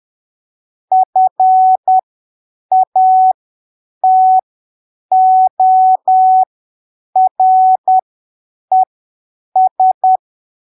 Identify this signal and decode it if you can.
Morse code, heard as a keyed tone, about 10 wpm, spaced as for 7 wpm, 750 Hz. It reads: FATORES
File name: arquivos / texto / texto 22 10ppm